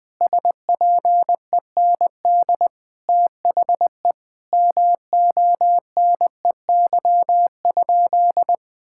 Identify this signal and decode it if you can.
Morse code, heard as a keyed tone, about 21 wpm, 700 Hz. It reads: SPEND THE MONEY?
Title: Sentences from Top 1000 Words - Review Part 5 - 20wpm